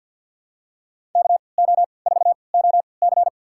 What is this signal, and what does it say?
Morse code, heard as a keyed tone, about 50 wpm, spaced as for 28 wpm, 700 Hz. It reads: KO4OC
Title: USA Call Signs - 5 Characters - 3x Character Spacing - 50wpm